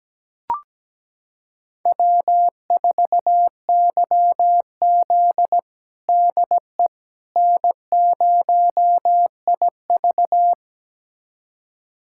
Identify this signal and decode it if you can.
Morse code, heard as a keyed tone, about 17 wpm, 700 Hz. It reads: W4YZ DE N0IV
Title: USA Callsign DE Callsign - 4 Characters - 17wpm